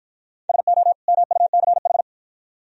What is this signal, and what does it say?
Morse code, heard as a keyed tone, about 53 wpm, 700 Hz. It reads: SO MUCH